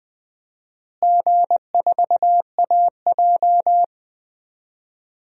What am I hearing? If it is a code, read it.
Morse code, heard as a keyed tone, about 20 wpm, 700 Hz. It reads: G4AJ
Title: DXCC-100 Call Signs - 4 Characters - Repeated 2x - 20wpm